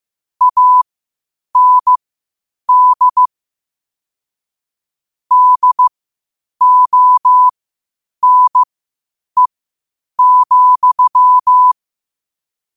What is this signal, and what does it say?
Morse code, heard as a keyed tone, about 15 wpm, spaced as for 8 wpm, 1000 Hz. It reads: AND DONE,
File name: qotdfiles / 8wpm